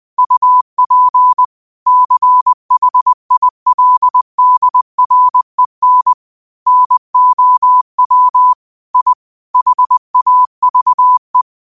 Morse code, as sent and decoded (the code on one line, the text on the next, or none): ..- .--. / -.-. .... .. .-.. -.. .-. -. / -. --- .-- / .. / .... .- ...- .
UP CHILDREN NOW I HAVE